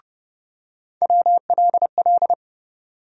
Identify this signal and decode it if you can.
Morse code, heard as a keyed tone, about 31 wpm, 700 Hz. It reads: WLL
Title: USA Call Sign Postfixes - 3 Letters 30wpm